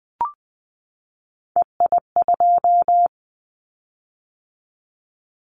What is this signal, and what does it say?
Morse code, heard as a keyed tone, about 20 wpm, 700 Hz. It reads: EI2
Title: Callsign Prefixes for Top 50 DXCC - 20wpm